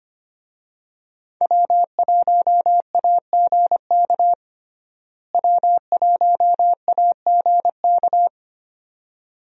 Morse code, heard as a keyed tone, about 25 wpm, 700 Hz. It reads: W1AGK W1AGK
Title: USA Call Signs - 5 Characters - Repeated 2x - 25wpm